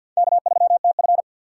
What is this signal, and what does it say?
Morse code, heard as a keyed tone, about 51 wpm, 700 Hz. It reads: K3TF
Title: K1USN SST QSO Practice - 50wpm